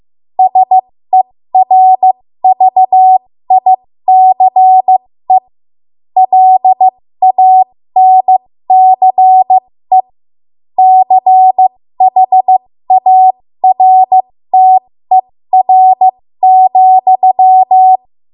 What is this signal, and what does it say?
Morse code, heard as a keyed tone, about 15 wpm, spaced as for 13 wpm, 750 Hz. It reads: SERVICE LANCE CHARTER,
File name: Morse / 13WPM